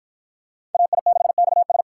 Morse code, heard as a keyed tone, about 52 wpm, 700 Hz. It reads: AI6XS